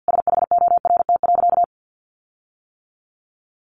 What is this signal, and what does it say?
Morse code, heard as a keyed tone, about 50 wpm, 700 Hz. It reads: SHORT.
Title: Sentences from Top 1400 Words 50wpm